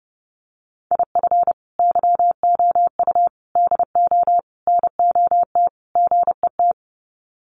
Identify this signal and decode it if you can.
Morse code, heard as a keyed tone, about 30 wpm, 700 Hz. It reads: IF YOU DO NOT GET
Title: Sets of 5 Words Encore2 30wpm